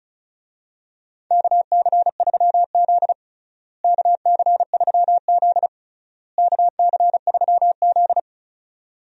Morse code, heard as a keyed tone, about 36 wpm, 700 Hz. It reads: KC3Z KC3Z KC3Z
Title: USA Call Signs - 4 Characters - Repeated 3x - 35wpm